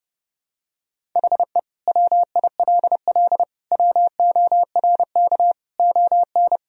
Morse code, heard as a keyed tone, about 30 wpm, 700 Hz. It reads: HE WILL WORK ON
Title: Sentences from Top 1000 Words - Review Part 1 - 30wpm